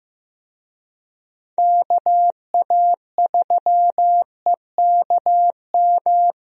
Morse code, heard as a keyed tone, about 15 wpm, 700 Hz. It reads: KA3EKM
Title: USA Call Signs - 6 Characters - 15wpm